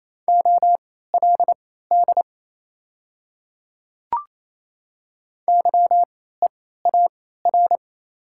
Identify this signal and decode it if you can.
Morse code, heard as a keyed tone, about 28 wpm, spaced as for 16 wpm, 700 Hz. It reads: OLD YEAR
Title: 3x Character Spacing - Top 200 Words - 28wpm